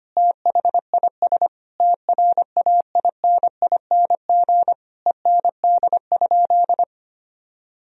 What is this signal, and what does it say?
Morse code, heard as a keyed tone, about 25 wpm, 700 Hz. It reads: THIS TRAINING END?
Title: Sentences from Top 1000 Words - Review Part 4 - 25wpm